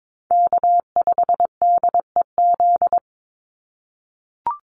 Morse code, heard as a keyed tone, about 22 wpm, 700 Hz. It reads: K5DEZ
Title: NA SOTA Chaser Callsigns 22wpm